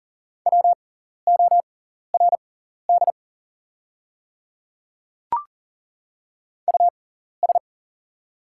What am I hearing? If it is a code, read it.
Morse code, heard as a keyed tone, about 41 wpm, spaced as for 14 wpm, 690 Hz. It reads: WORD US
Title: 6x Character Spacing - Top 300 Words - 40wpm